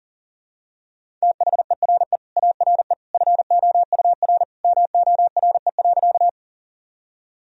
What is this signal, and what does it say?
Morse code, heard as a keyed tone, about 40 wpm, 700 Hz. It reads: THERE ARE FOUR MORE.
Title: Sentences from Top 1000 Words - Review Part 1 - 40wpm